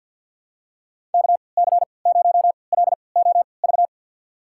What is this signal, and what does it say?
Morse code, heard as a keyed tone, about 52 wpm, spaced as for 28 wpm, 700 Hz. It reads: KC0LOV
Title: USA Call Signs - 6 Characters - 3x Character Spacing - 50wpm